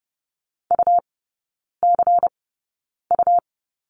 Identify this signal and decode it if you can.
Morse code, heard as a keyed tone, about 30 wpm, spaced as for 9 wpm, 700 Hz. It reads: UCU
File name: Lesson 17e - Letter u - Three Characters 30wpm